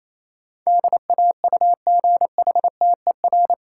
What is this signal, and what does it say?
Morse code, heard as a keyed tone, about 28 wpm, 700 Hz. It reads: DAUGHTER
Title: Sentences from Top 1500 Words - New Word Drill - 28wpm